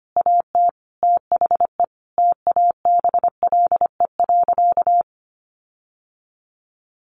Morse code, heard as a keyed tone, about 25 wpm, 700 Hz. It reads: AT THE TABLE.
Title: Sentences from Top 1500 Words 25wpm